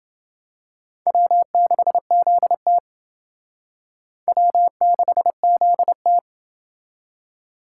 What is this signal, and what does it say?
Morse code, heard as a keyed tone, about 28 wpm, 700 Hz. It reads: W6ZT W6ZT